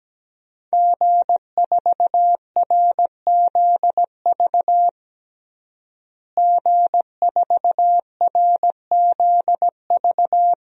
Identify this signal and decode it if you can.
Morse code, heard as a keyed tone, about 17 wpm, 700 Hz. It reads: G4RZV G4RZV
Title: DXCC-100 Call Signs - 5 Characters - Repeated 3x - 17wpm